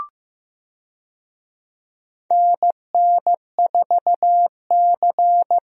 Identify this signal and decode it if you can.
Morse code, heard as a keyed tone, about 15 wpm, 700 Hz. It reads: NN4C
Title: USA Call Signs - 4 Characters - Repeated 3x - 15wpm